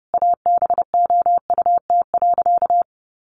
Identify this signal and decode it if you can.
Morse code, heard as a keyed tone, about 28 wpm, 700 Hz. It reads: ABOUT.
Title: Sentences from Top 800 Words 30wpm